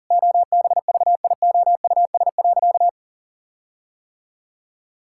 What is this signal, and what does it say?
Morse code, heard as a keyed tone, about 40 wpm, 700 Hz. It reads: OBVIOUS.
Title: Sentences from Top 1600 Words 40wpm